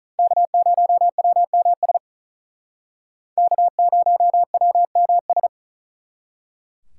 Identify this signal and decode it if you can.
Morse code, heard as a keyed tone, about 46 wpm, 700 Hz. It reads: K0WMS K0WMS